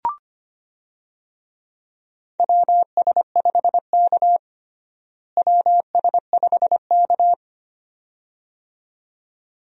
Morse code, heard as a keyed tone, about 25 wpm, 700 Hz. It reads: WS5K WS5K
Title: USA Call Signs - 4 Characters - Repeated 2x - 25wpm